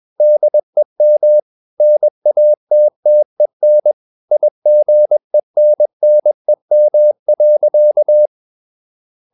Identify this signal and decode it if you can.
Morse code, heard as a keyed tone, about 21 wpm, 600 Hz. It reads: DEM NATTEN IGENNEM.
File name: Ferie QRS 21wpm